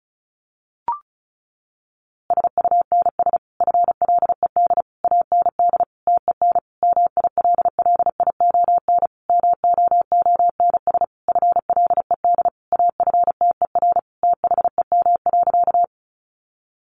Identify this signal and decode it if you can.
Morse code, heard as a keyed tone, about 34 wpm, 700 Hz. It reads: SUNS FLED AND TEN MILLION MOONS FLED AFTER THEM.
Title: A Sound Of Thunder 35wpm